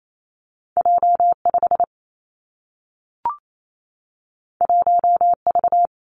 Morse code, heard as a keyed tone, about 28 wpm, 700 Hz. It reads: J 5 1 V
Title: Two Challenging Letter-Number Combinations 28wpm